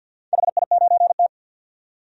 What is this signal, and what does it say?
Morse code, heard as a keyed tone, about 51 wpm, 700 Hz. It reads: HI9T
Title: DXCC-100 Call Signs - 4 Characters - 50wpm